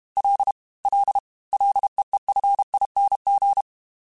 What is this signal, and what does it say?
Morse code, heard as a keyed tone, about 32 wpm, 800 Hz. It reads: L L LEEFING
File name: F and L / F and L mp3 / F and L Prefixes 3x2 Twice @30